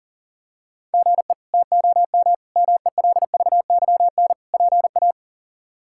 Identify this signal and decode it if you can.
Morse code, heard as a keyed tone, about 37 wpm, 700 Hz. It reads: GE TOM MELVYN PA
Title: K1USN SST QSO Practice - 40wpm